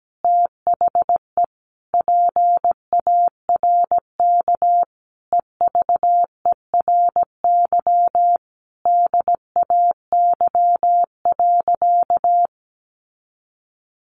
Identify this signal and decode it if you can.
Morse code, heard as a keyed tone, about 17 wpm, 700 Hz. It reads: THE PARK EVERY DAY.
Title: Sentences from Top 1300 Words 17wpm